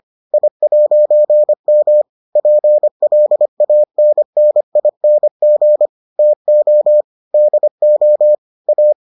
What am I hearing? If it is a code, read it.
Morse code, heard as a keyed tone, about 25 wpm, 600 Hz. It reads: I'M PLANNING TO DO A